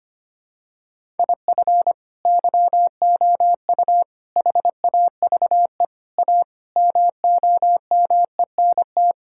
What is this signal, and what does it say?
Morse code, heard as a keyed tone, about 25 wpm, 700 Hz. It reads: IF YOU HAVE A MOMENT
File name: practice / 25wpm / Sentences with 4-Word Phrases - 25wpm